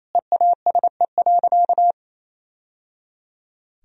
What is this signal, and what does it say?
Morse code, heard as a keyed tone, about 27 wpm, 700 Hz. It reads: EASE.
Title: Sentences from Top 1000 Words - Review Part 1 - 28wpm